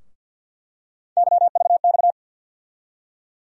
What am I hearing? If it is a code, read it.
Morse code, heard as a keyed tone, about 52 wpm, 700 Hz. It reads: YVX
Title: USA Call Sign Postfixes - 3 Letters 50wpm